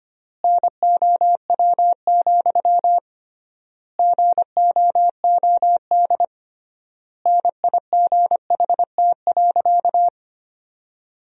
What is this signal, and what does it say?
Morse code, heard as a keyed tone, about 25 wpm, 700 Hz. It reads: NOW, GOOD NIGHT.
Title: 3x Word-Spacing - Sentences from Top 300 Words - 25wpm